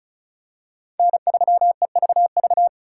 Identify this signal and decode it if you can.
Morse code, heard as a keyed tone, about 36 wpm, 700 Hz. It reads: N3EVV
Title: USA Call Signs - 5 Characters - 2x Character Spacing - 35wpm